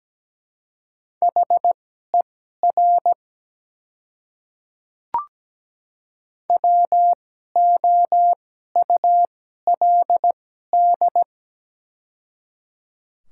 Morse code, heard as a keyed tone, about 17 wpm, spaced as for 12 wpm, 700 Hz. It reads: HER WOULD
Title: 2x Character Spacing - Top 100 Words - 17wpm